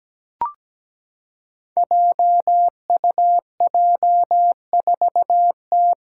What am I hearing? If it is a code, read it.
Morse code, heard as a keyed tone, about 17 wpm, 700 Hz. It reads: JUJ4T
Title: Five Challenging Letter-Number Combinations 17wpm